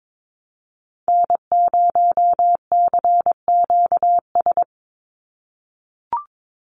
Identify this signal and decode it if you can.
Morse code, heard as a keyed tone, about 22 wpm, 700 Hz. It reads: N0CQS
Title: USA Call Signs 22wpm